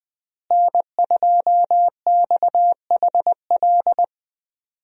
Morse code, heard as a keyed tone, about 20 wpm, 700 Hz. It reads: N2XHL
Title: USA Call Signs - 5 Characters - Repeated 2x - 20wpm